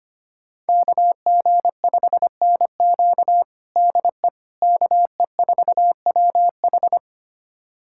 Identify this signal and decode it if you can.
Morse code, heard as a keyed tone, about 25 wpm, 700 Hz. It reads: KG5NQ DE KE4WH
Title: USA Callsign DE Callsign 25wpm